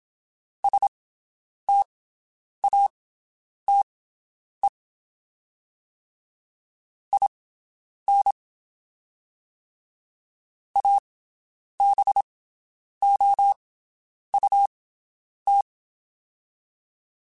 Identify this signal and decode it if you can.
Morse code, heard as a keyed tone, about 27 wpm, spaced as for 9 wpm, 800 Hz. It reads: STATE IN ABOUT
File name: (spaced) / QSOs Spaced-mp3s / WD0BGP 1 @25 sp